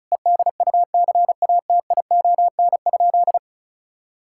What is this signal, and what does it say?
Morse code, heard as a keyed tone, about 36 wpm, 700 Hz. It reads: EDUCATION?
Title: Sentences from Top 1600 Words 35wpm